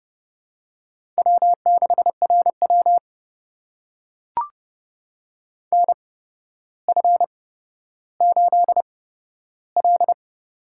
Morse code, heard as a keyed tone, about 30 wpm, spaced as for 13 wpm, 700 Hz. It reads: W6RW NF8L
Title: USA Call Signs - 8x Character Spacing - 30wpm